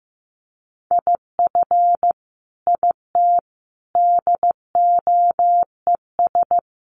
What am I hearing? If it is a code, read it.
Morse code, heard as a keyed tone, about 15 wpm, 700 Hz. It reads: IF IT DOES
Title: Sentences from Top 1300 Words 15wpm